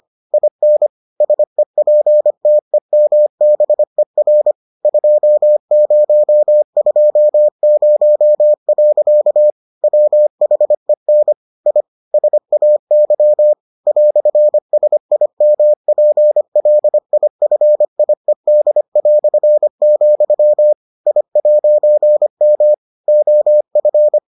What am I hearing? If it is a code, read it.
Morse code, heard as a keyed tone, about 25 wpm, 600 Hz. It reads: IN SEPTEMBER 2020. WHEN I SAY "SIMPLIFIED", I'M OF